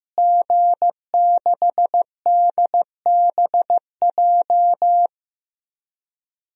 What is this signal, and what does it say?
Morse code, heard as a keyed tone, about 15 wpm, 700 Hz. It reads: G6DBJ
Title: DXCC-100 Call Signs - 5 Characters - Repeated 3x - 15wpm